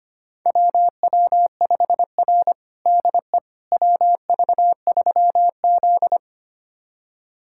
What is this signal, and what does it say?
Morse code, heard as a keyed tone, about 25 wpm, 700 Hz. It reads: WW5R DE WV3Z
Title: USA Callsign DE Callsign - 4 Characters - 25wpm